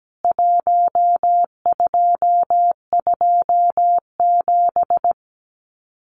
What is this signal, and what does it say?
Morse code, heard as a keyed tone, about 17 wpm, 700 Hz. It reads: 1227
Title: CWT Exchange 17wpm